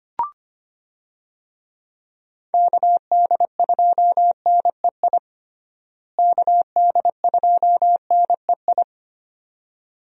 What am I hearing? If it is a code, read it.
Morse code, heard as a keyed tone, about 25 wpm, 700 Hz. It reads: KD2NEI KD2NEI
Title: USA Call Signs - 6 Characters - Repeated 2x - 25wpm